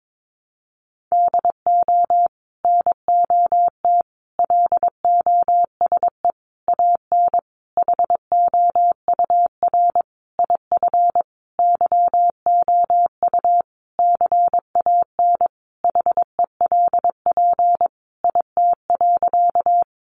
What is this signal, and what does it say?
Morse code, heard as a keyed tone, about 22 wpm, 700 Hz. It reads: DO NOT LOSE AN HOUR IF YOU CAN HELP IT.
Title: Sentences from Top 1100 Words 22wpm